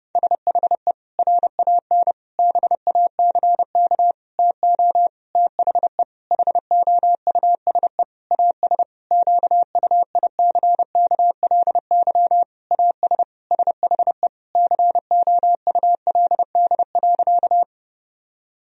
Morse code, heard as a keyed tone, about 30 wpm, 700 Hz. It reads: SHE RAN BACK TO THE HOUSE AS QUICKLY AS SHE COULD.
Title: Sentences from Top 1000 Words - Review Part 3 - 30wpm